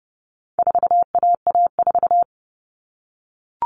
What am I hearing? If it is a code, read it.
Morse code, heard as a keyed tone, about 30 wpm, 700 Hz. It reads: 4AA4
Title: All Cut Numbers - 4-Digit 30wpm